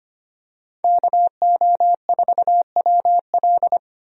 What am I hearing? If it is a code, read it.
Morse code, heard as a keyed tone, about 25 wpm, 700 Hz. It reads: KO4WL
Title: USA Call Signs - 5 Characters - 25wpm